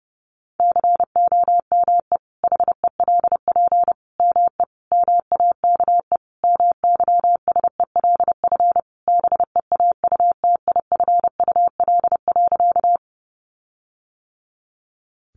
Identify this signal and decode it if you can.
Morse code, heard as a keyed tone, about 30 wpm, 700 Hz. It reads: COME HELP ME MAKE MYSELF BEAUTIFUL.
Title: Sentences from Top 1300 Words 30wpm